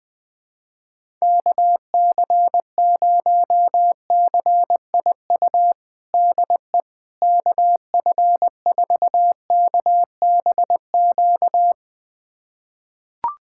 Morse code, heard as a keyed tone, about 21 wpm, 700 Hz. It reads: KC0CIU DE KF4KBQ
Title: USA Callsign DE Callsign - 6 Characters - 20wpm